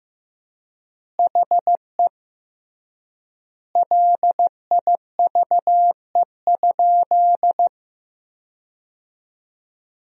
Morse code, heard as a keyed tone, about 15 wpm, 700 Hz. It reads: HE LIVE?
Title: 3x Word-Spacing - Sentences from Top 200 Words - 15wpm